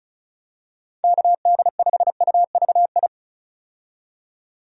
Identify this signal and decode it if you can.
Morse code, heard as a keyed tone, about 36 wpm, 700 Hz. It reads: KD5UVI
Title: USA Call Signs - 6 Characters - 35wpm